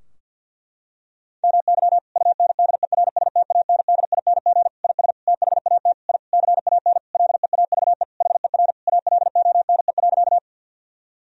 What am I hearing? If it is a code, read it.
Morse code, heard as a keyed tone, about 51 wpm, 700 Hz. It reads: MY UNDERSTANDING IS THAT I CAN LEAVE HER ALONE.
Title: Sentences from Top 1000 Words - Review Part 3 - 50wpm